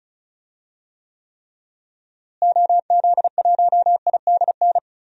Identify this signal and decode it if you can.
Morse code, heard as a keyed tone, about 36 wpm, 700 Hz. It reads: OZ1IDN
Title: DXCC-100 Call Signs - 6 Characters - Repeated 3x - 35wpm